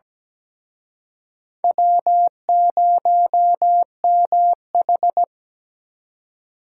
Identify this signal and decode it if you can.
Morse code, heard as a keyed tone, about 20 wpm, 700 Hz. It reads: W0MH
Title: Speed Racing - USA 1x2 Call Signs 17wpm